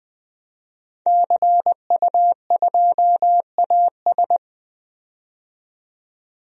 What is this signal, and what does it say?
Morse code, heard as a keyed tone, about 20 wpm, 700 Hz. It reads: CU2AS
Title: DXCC-100 Call Signs - 5 Characters - 20wpm